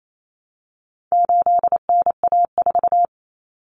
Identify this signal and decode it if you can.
Morse code, heard as a keyed tone, about 28 wpm, 700 Hz. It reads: 8NA4
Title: Common Cut Numbers - 4-Digit 28wpm